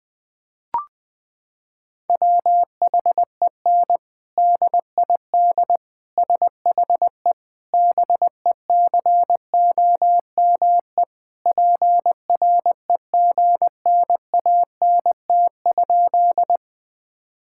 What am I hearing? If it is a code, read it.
Morse code, heard as a keyed tone, about 20 wpm, 700 Hz. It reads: WHEN DID SHE BECOME PREGNANT?
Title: Sentences from Top 1000 Words - Review Part 6 - 20wpm